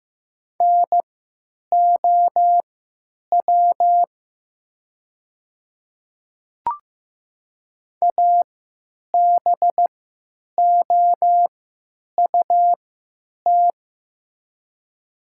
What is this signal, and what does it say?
Morse code, heard as a keyed tone, about 15 wpm, spaced as for 9 wpm, 700 Hz. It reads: NOW ABOUT
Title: 3x Character Spacing - Top 100 Words - 15wpm